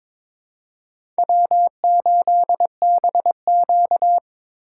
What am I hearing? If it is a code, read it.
Morse code, heard as a keyed tone, about 21 wpm, 700 Hz. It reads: W8BQ
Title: USA Call Signs - 4x Character Spacing - 22wpm